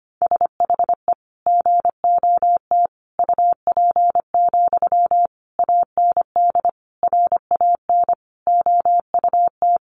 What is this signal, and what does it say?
Morse code, heard as a keyed tone, about 25 wpm, 700 Hz. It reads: SHE GOT UP, AND RAN OUT
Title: Sentences from Top 500 Words 25wpm